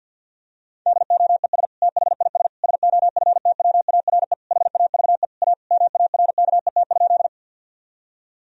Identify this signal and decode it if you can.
Morse code, heard as a keyed tone, about 51 wpm, 700 Hz. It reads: DOES THIS SOFTWARE HAVE A MARKET?
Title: Sentences from Top 1000 Words - Review Part 3 - 50wpm